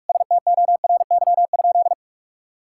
Morse code, heard as a keyed tone, about 41 wpm, 700 Hz. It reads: STORY?
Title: Sentences from Top 1000 Words - Review Part 5 - 45wpm